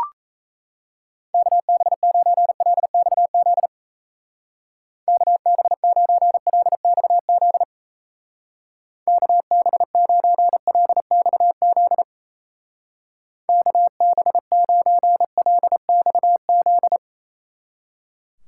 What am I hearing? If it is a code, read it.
Morse code, heard as a keyed tone, about 36 wpm, 700 Hz. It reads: KB9LXZ KB9LXZ KB9LXZ KB9LXZ